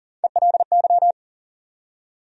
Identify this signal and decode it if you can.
Morse code, heard as a keyed tone, about 40 wpm, 700 Hz. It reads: ELY
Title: USA Call Sign Postfixes - 3 Letters 40wpm